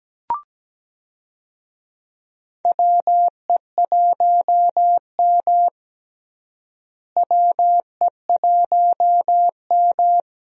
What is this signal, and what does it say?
Morse code, heard as a keyed tone, about 17 wpm, 700 Hz. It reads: WE1M WE1M
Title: USA Call Signs - 4 Characters - Repeated 3x - 17wpm